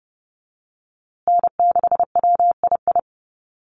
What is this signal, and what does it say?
Morse code, heard as a keyed tone, about 30 wpm, 700 Hz. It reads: N6WII
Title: USA Call Signs Repeated 3x 30wpm